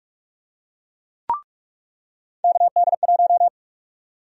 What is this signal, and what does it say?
Morse code, heard as a keyed tone, about 41 wpm, 700 Hz. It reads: KD1
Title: USA Call Sign Prefixes - 45wpm